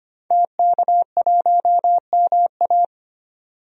Morse code, heard as a keyed tone, about 25 wpm, 700 Hz. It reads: TK1MA
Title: DXCC-100 Call Signs - 5 Characters - 25wpm